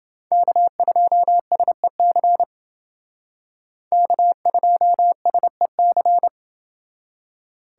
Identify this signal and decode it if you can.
Morse code, heard as a keyed tone, about 28 wpm, 700 Hz. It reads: K2SEC K2SEC